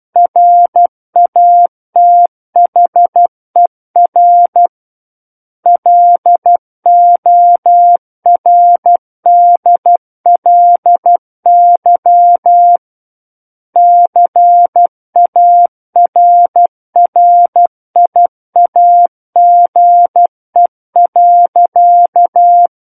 Morse code, heard as a keyed tone, about 12 wpm, 700 Hz. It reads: RATHER LORDLY CARRIAGE.
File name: Book One / 03-Chapter 3 - 12 WPM